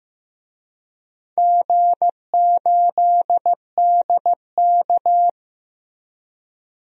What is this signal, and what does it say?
Morse code, heard as a keyed tone, about 15 wpm, 700 Hz. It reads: G8DK
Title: DXCC-100 Call Signs - 4 Characters - Repeated 3x - 15wpm